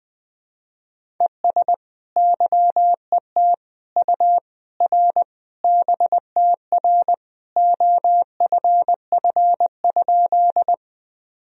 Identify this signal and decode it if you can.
Morse code, heard as a keyed tone, about 20 wpm, 700 Hz. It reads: ES YET U R BTR OFF?
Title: Sentences from Top 500 Words and Abbreviations 20wpm